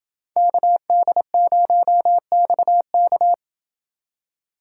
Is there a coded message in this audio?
Yes: Morse code from 0.4 to 3.3 s, heard as a keyed tone, about 28 wpm, 700 Hz. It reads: KD0XK